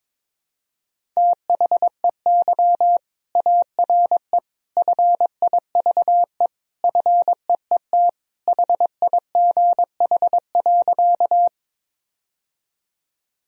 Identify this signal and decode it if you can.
Morse code, heard as a keyed tone, about 22 wpm, 700 Hz. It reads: THEY ARE FIVE FEET HIGH.
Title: Sentences from Top 1000 Words - Review Part 2 - 22wpm